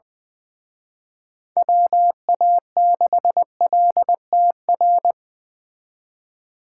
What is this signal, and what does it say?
Morse code, heard as a keyed tone, about 24 wpm, 700 Hz. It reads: WA6LTR
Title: Speed Racing - USA Call Signs 20wpm